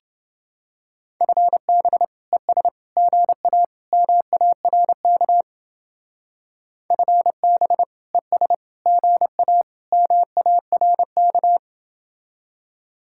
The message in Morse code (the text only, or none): FB ES GA MARK FB ES GA MARK